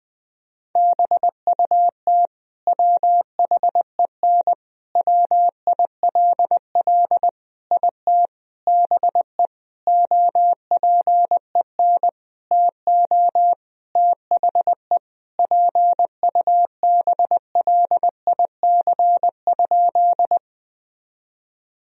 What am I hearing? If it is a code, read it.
Morse code, heard as a keyed tone, about 20 wpm, 700 Hz. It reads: BUT WHEN WILL IT BE OPEN TO THE PUBLIC?
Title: Sentences from Top 1000 Words - Review Part 3 - 20wpm